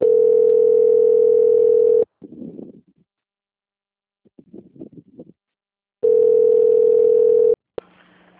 Ringback tone (North America), 440 + 480 Hz.
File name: out- - - - - .47430